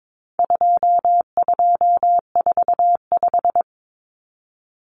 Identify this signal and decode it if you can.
Morse code, heard as a keyed tone, about 22 wpm, 700 Hz. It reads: 2245
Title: Numbers - 4-Digit 22wpm